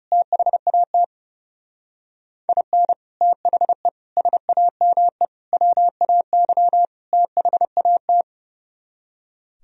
Morse code, heard as a keyed tone, about 34 wpm, 700 Hz. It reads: THAT IN THE SAME WAY THAT
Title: Speed-Racing - Sets of 5 Words 30wpm